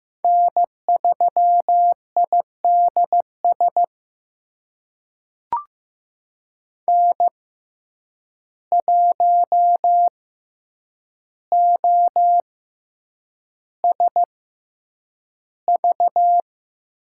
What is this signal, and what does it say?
Morse code, heard as a keyed tone, about 15 wpm, spaced as for 8 wpm, 700 Hz. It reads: N3IDS N1OSV
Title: USA Call Signs - 5 Characters - 6x Character Spacing - 15wpm